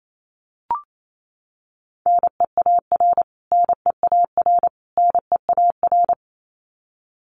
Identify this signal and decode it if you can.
Morse code, heard as a keyed tone, about 28 wpm, 700 Hz. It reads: NEAR NEAR NEAR
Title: Top 500 Words Repeated 28wpm